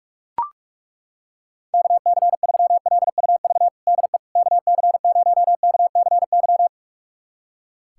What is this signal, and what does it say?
Morse code, heard as a keyed tone, about 43 wpm, 700 Hz. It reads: KC3LUV DE KC0KCY
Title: USA Callsign DE Callsign 45wpm